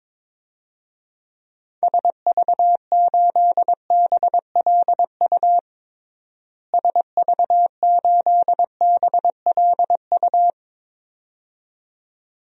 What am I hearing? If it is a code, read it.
Morse code, heard as a keyed tone, about 22 wpm, 700 Hz. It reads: SV8BLU SV8BLU
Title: DXCC-100 Call Signs - 6 Characters - Repeated 2x - 22wpm